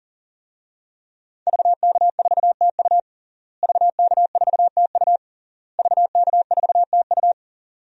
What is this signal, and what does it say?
Morse code, heard as a keyed tone, about 37 wpm, 700 Hz. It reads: VK4TU VK4TU VK4TU
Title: DXCC-100 Call Signs - 5 Characters - Repeated 3x - 40wpm